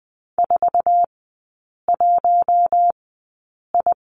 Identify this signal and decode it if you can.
Morse code, heard as a keyed tone, about 20 wpm, spaced as for 8 wpm, 700 Hz. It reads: 41I
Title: 053 Lesson 10e - Letter r - Three Characters 20wpm